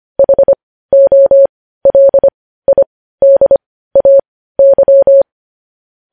Morse code, heard as a keyed tone, about 25 wpm, spaced as for 15 wpm, 550 Hz. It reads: HOLIDAY
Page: News Headlines in Morse Code at 15 WPM